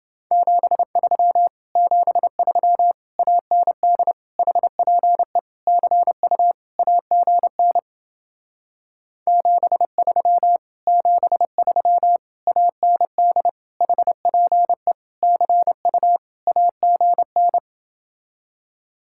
Morse code, heard as a keyed tone, about 30 wpm, 700 Hz. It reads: 73 73 AND HPE CU AGN 73 73 AND HPE CU AGN